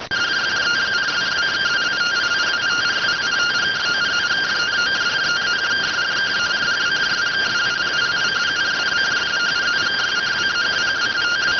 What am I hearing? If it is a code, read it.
Two-tone teleprinter signal, about 170 Hz shift.